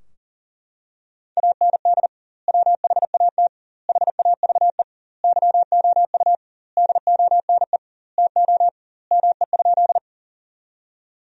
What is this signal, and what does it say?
Morse code, heard as a keyed tone, about 40 wpm, 700 Hz. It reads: AND WHAT HAVE YOU DONE TO ME?
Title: 2x Word-Spacing - Sentences from Top 500 Words - 40wpm